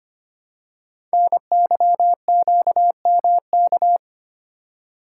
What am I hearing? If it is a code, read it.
Morse code, heard as a keyed tone, about 25 wpm, 700 Hz. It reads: NYQMK
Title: Code Groups - 5 Letters - 25wpm